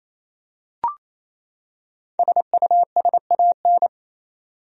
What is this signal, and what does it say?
Morse code, heard as a keyed tone, about 28 wpm, 700 Hz. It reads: SUSAN
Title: USA Common Names - 28wpm